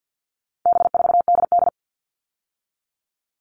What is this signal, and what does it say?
Morse code, heard as a keyed tone, about 50 wpm, 700 Hz. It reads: B4DD
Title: All Cut Numbers - 4-Digit 50wpm